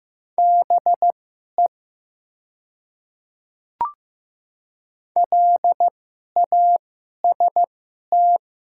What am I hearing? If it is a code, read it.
Morse code, heard as a keyed tone, about 15 wpm, spaced as for 11 wpm, 700 Hz. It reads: BE LAST
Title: 2x Character Spacing - Top 300 Words - 15wpm